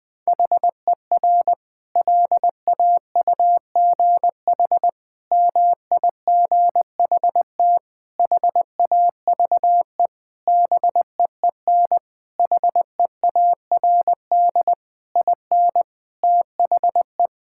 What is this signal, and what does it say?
Morse code, heard as a keyed tone, about 21 wpm, 700 Hz. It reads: HER LAUGH MIGHT HAVE BEEN HEARD IN THE
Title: Sentences from Top 1000 Words - Review Part 5 - 20wpm